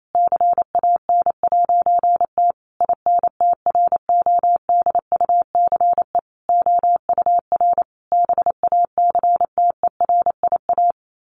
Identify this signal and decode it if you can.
Morse code, heard as a keyed tone, about 28 wpm, 700 Hz. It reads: CAN'T INTRODUCE OUR BACTERIA
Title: A Sound Of Thunder 28wpm